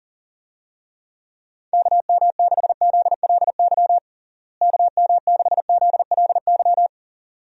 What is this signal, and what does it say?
Morse code, heard as a keyed tone, about 40 wpm, 700 Hz. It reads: KM6ZLY KM6ZLY
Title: USA Call Signs - 6 Characters - Repeated 2x - 40wpm